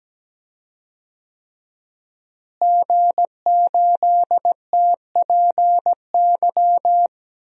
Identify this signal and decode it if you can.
Morse code, heard as a keyed tone, about 17 wpm, 700 Hz. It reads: G8TPY
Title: DXCC-100 Call Signs - 5 Characters - Repeated 3x - 17wpm